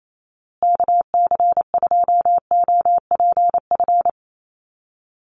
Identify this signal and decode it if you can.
Morse code, heard as a keyed tone, about 28 wpm, 700 Hz. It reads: KC2OPF
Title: USA Call Signs Repeated 2x 28wpm